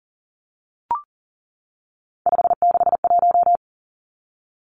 Morse code, heard as a keyed tone, about 37 wpm, 700 Hz. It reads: 561
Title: Numbers - 3-Digit 40wpm